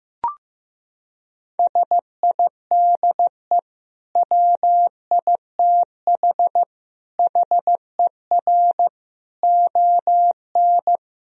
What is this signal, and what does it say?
Morse code, heard as a keyed tone, about 15 wpm, 700 Hz. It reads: SIDE WITH HER ON
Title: Sentences from Top 1000 Words - Review Part 1 - 15wpm